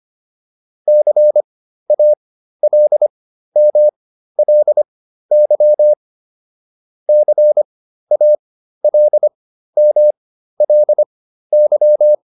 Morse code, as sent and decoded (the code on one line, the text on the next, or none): -.-. .- .-.. -- .-.. -.-- / -.-. .- .-.. -- .-.. -.--
CALMLY CALMLY